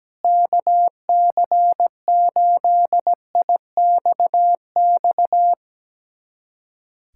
Morse code, heard as a keyed tone, about 17 wpm, 700 Hz. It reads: KC8IXX
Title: USA Callsign DE Callsign 17wpm